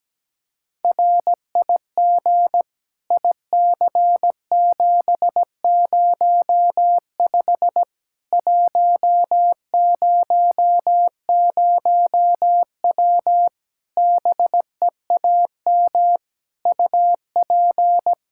Morse code, heard as a keyed tone, about 17 wpm, 700 Hz. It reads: RIG IC705 100W BEAM UP
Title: K1USN SST QSO Practice 17wpm